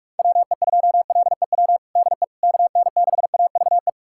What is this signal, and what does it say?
Morse code, heard as a keyed tone, about 46 wpm, 700 Hz. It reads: WE2LEW DE KN6AVE
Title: USA Callsign DE Callsign - 6 Characters - 45wpm